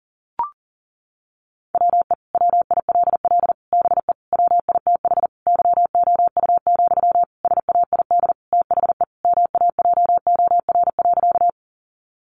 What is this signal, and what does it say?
Morse code, heard as a keyed tone, about 40 wpm, 700 Hz. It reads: WE WILL BE WITH YOU, SAID THE MAJOR.
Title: Sentences from Top 600 Words 40wpm